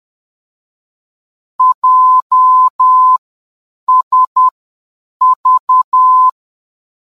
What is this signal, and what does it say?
Morse code, heard as a keyed tone, about 10 wpm, spaced as for 7 wpm, 1020 Hz. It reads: JSV
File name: JSV-VOR DME